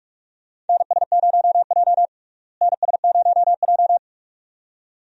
Morse code, heard as a keyed tone, about 45 wpm, 700 Hz. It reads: NS0J NS0J
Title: USA Call Signs - 4 Characters - Repeated 2x - 45wpm